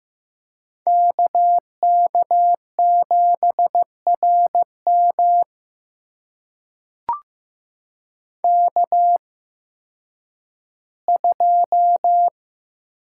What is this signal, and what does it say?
Morse code, heard as a keyed tone, about 15 wpm, spaced as for 10 wpm, 700 Hz. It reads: KK7RM K2
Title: USA Call Signs - 5 Characters - 8x Character Spacing - 15wpm